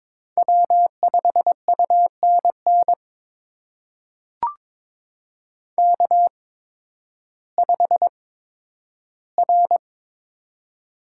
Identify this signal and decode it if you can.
Morse code, heard as a keyed tone, about 22 wpm, spaced as for 12 wpm, 700 Hz. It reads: W5UNN K5R
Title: USA Call Signs - 5 Characters - 8x Character Spacing - 22wpm